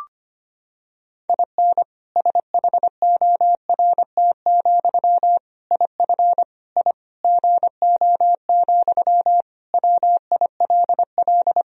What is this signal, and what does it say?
Morse code, heard as a keyed tone, about 25 wpm, 700 Hz. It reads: IN SHORT, IF I GO, WILL
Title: Sentences from Top 1000 Words - Review Part 2 - 25wpm